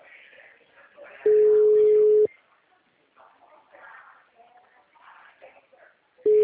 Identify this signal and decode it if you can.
Ringback tone (Europe, 425 Hz), with speech.
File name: out- -2016- - - .40458